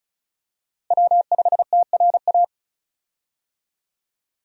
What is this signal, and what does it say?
Morse code, heard as a keyed tone, about 36 wpm, 700 Hz. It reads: W5TRA
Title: USA Call Signs - 5 Characters - Repeated 2x - 35wpm